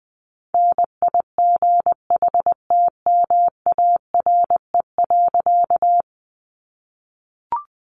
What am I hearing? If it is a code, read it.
Morse code, heard as a keyed tone, about 20 wpm, 700 Hz. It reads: NIGHTMARE.
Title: A Sound Of Thunder 20wpm